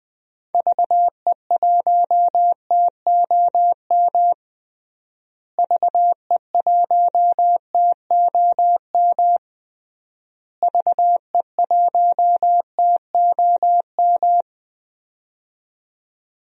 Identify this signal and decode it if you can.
Morse code, heard as a keyed tone, about 19 wpm, 700 Hz. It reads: VE1TOM VE1TOM VE1TOM
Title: DXCC-100 Call Signs - 6 Characters - Repeated 3x - 20wpm